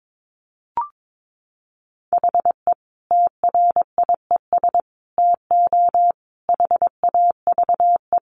Morse code, heard as a keyed tone, about 22 wpm, 700 Hz. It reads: HE TRIES TO HAVE
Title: Easy Phrases 22wpm